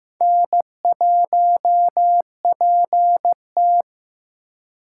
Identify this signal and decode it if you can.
Morse code, heard as a keyed tone, about 15 wpm, 700 Hz. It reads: N1PT
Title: USA Call Signs - 6x Character Spacing - 15wpm